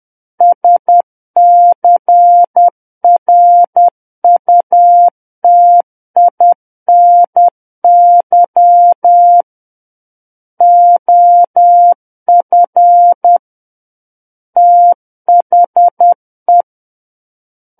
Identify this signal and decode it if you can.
Morse code, heard as a keyed tone, about 10 wpm, 700 Hz. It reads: SCRUTINY OF THE
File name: Book One / 01-Chapter 1 - 10 WPM